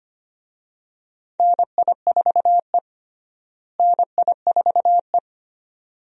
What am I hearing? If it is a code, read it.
Morse code, heard as a keyed tone, about 25 wpm, 700 Hz. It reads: NI4E NI4E
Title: USA Call Signs - 4 Characters - Repeated 2x - 25wpm